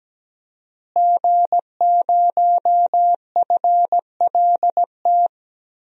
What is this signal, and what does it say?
Morse code, heard as a keyed tone, about 17 wpm, 700 Hz. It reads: G0FLT
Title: DXCC-100 Call Signs - 5 Characters - Repeated 2x - 17wpm